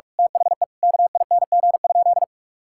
Morse code, heard as a keyed tone, about 41 wpm, 700 Hz. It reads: THE KING?
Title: Sentences from Top 1000 Words - Review Part 5 - 45wpm